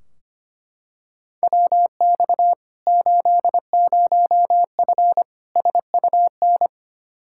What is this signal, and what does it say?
Morse code, heard as a keyed tone, about 25 wpm, 700 Hz. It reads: WX 80F SUN
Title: Practice QSO - Exchange 2 - Info - 25wpm